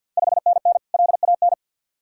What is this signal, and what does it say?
Morse code, heard as a keyed tone, about 53 wpm, 690 Hz. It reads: 5NN LAN
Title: Ontario QSO Party Exchange 50wpm